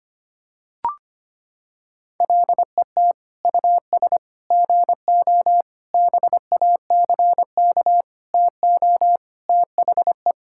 Morse code, heard as a keyed tone, about 25 wpm, 700 Hz. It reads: LET US GO BACK TO THE
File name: practice / 25wpm / Sentences with 4-Word Phrases - 25wpm